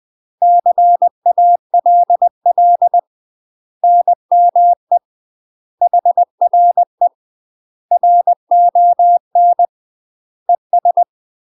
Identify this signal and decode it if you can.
Morse code, heard as a keyed tone, about 20 wpm, 700 Hz. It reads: CALL NME HRE RON ES
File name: UK MASTER QSOs TRAINING / 20WPM / UK MASTER QSOs 0029